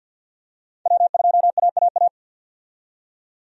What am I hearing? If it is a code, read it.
Morse code, heard as a keyed tone, about 50 wpm, 700 Hz. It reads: W2AAA
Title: USA Call Signs - 5 Characters - 50wpm